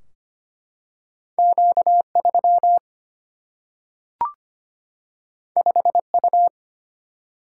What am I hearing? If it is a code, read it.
Morse code, heard as a keyed tone, about 25 wpm, 700 Hz. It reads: Q3 5U
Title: Single-Letter Single-Number 25wpm